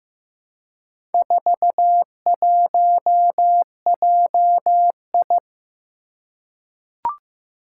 Morse code, heard as a keyed tone, about 15 wpm, 700 Hz. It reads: 41JI
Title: Four Challenging Letter-Number Combinations 15wpm